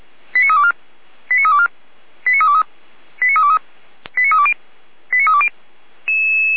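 Five-tone selective call (ZVEI), heard as quick runs of five tones, tones of about 70 ms. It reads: 89324 89324 89323 89323 89320 89320
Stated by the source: ZVEI five-tone selective call received on SDR.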